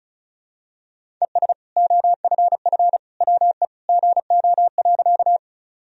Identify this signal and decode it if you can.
Morse code, heard as a keyed tone, about 36 wpm, 700 Hz. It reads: ES OFF WE GO.
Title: Sentences from Top 200 Words and Abbreviations 35wpm